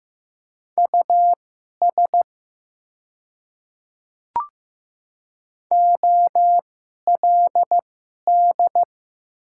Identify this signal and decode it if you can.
Morse code, heard as a keyed tone, about 15 wpm, spaced as for 11 wpm, 700 Hz. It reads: US OLD
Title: 2x Character Spacing - Top 200 Words - 15wpm